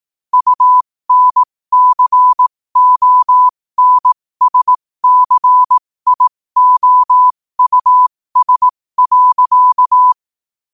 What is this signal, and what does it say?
Morse code, heard as a keyed tone, about 18 wpm, spaced as for 15 wpm, 1000 Hz. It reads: UNCONSCIOUS.
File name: qotdfiles / 15wpm